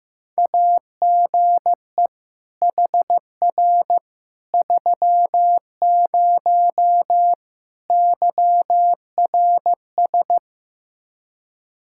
Morse code, heard as a keyed tone, about 15 wpm, 700 Hz. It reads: AGE HR 30 YRS
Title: Practice QSO - Exchange 3 - More Info - 15wpm